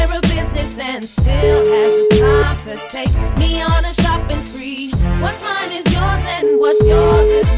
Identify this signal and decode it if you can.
Ringback tone (Europe, 425 Hz), with noise.